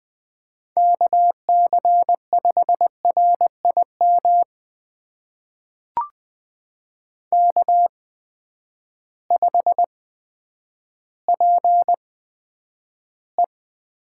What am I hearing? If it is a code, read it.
Morse code, heard as a keyed tone, about 20 wpm, spaced as for 10 wpm, 700 Hz. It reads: KC5RIM K5PE
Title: USA Call Signs - 8x Character Spacing - 20wpm